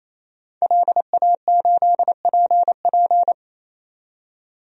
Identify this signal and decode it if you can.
Morse code, heard as a keyed tone, about 28 wpm, 700 Hz. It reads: LA8PP
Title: DXCC-100 Call Signs - 5 Characters - 28wpm